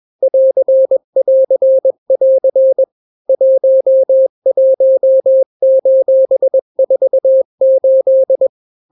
Morse code, heard as a keyed tone, about 21 wpm, 530 Hz. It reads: <AR><AR><AR> 11:48
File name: daily / 20wpm